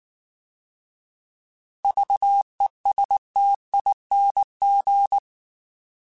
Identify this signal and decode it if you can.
Morse code, heard as a keyed tone, about 19 wpm, 800 Hz. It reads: VESTING